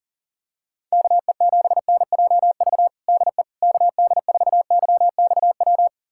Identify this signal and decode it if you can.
Morse code, heard as a keyed tone, about 40 wpm, 700 Hz. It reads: KE7NJV DE KD4YXW
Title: USA Callsign DE Callsign 40wpm